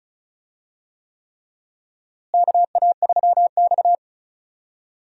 Morse code, heard as a keyed tone, about 36 wpm, 700 Hz. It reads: KA3X
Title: USA Call Signs - 4 Characters - Repeated 3x - 35wpm